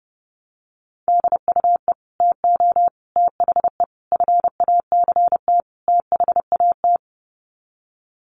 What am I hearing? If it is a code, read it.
Morse code, heard as a keyed tone, about 30 wpm, 700 Hz. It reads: DUE TO THE FACT THAT
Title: Sets of 5 Words Encore 30wpm